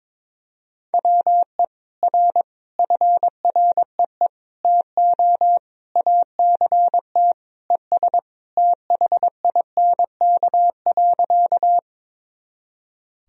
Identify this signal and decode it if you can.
Morse code, heard as a keyed tone, about 22 wpm, 700 Hz. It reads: WE R FREE TO ACT ES THINK.
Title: Sentences from Top 500 Words and Abbreviations 22wpm